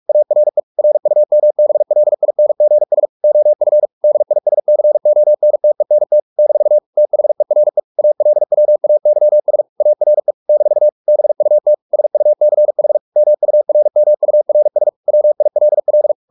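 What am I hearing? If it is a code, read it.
Morse code, heard as a keyed tone, about 44 wpm, 600 Hz. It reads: ARE RUMBLINGS OF DISCONTENT - THERE ALWAYS ARE - BUT SUCH MURMURS WILL
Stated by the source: Morse practice at 45 wpm, the novel Never by Ken Follett in English, from CW med Gnister.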